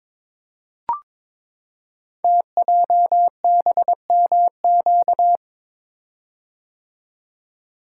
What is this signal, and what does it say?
Morse code, heard as a keyed tone, about 23 wpm, 700 Hz. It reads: TJBMQ
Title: Code Groups - 5 Letters - 22wpm